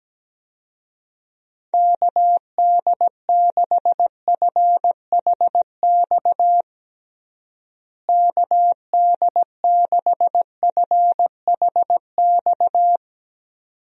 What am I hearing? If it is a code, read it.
Morse code, heard as a keyed tone, about 17 wpm, 700 Hz. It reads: KD6FHX KD6FHX
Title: USA Call Signs - 6 Characters - Repeated 2x - 17wpm